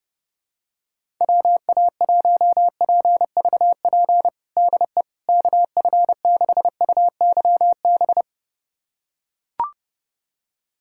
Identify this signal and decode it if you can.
Morse code, heard as a keyed tone, about 32 wpm, 700 Hz. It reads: WA1PVP DE KF6UYB
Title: USA Callsign DE Callsign - 6 Characters - 30wpm